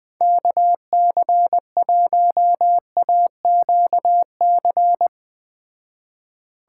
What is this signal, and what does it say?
Morse code, heard as a keyed tone, about 20 wpm, 700 Hz. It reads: KC1AQC